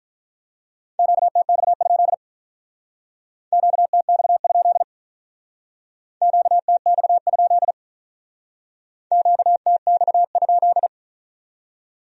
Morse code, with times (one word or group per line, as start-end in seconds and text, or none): 1.0-2.1 QTX?
3.5-4.8 QTX?
6.2-7.7 QTX?
9.1-10.9 QTX?